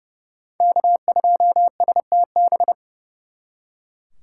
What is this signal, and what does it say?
Morse code, heard as a keyed tone, about 35 wpm, 700 Hz. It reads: K2STB
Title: Speed Racing - USA 1x3 Call Signs 30wpm